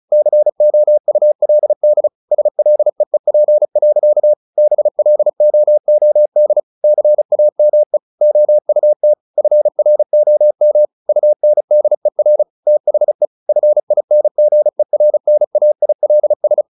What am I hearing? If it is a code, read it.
Morse code, heard as a keyed tone, about 34 wpm, 600 Hz. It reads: COULD SLEEP. BLOOD CAME OUT FROM UNDER THE FINGERNAILS
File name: the old man and the sea 2 35wpm